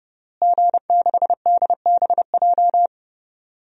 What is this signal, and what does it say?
Morse code, heard as a keyed tone, about 30 wpm, 700 Hz. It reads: G6DBJ
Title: DXCC-100 Call Signs - 5 Characters - Repeated 3x - 30wpm